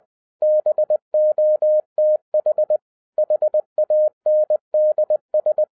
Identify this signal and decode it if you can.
Morse code, heard as a keyed tone, about 20 wpm, 610 Hz. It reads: BOTH HANDS
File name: 2011-07-mp3 / n58 BACK FROM A BIG ACCIDENT 20wpm